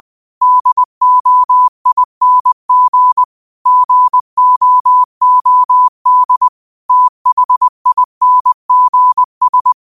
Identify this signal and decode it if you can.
Morse code, heard as a keyed tone, about 20 wpm, 1000 Hz. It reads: DOING GOOD THINGS
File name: qotdfiles / 20wpm